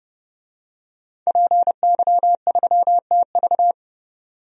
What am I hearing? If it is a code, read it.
Morse code, heard as a keyed tone, about 30 wpm, 700 Hz. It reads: PY3TV
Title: DXCC-100 Call Signs - 5 Characters - 30wpm